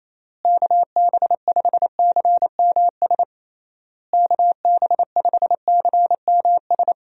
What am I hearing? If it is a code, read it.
Morse code, heard as a keyed tone, about 28 wpm, 700 Hz. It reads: KB5CMS KB5CMS
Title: USA Call Signs - 6 Characters - Repeated 3x - 28wpm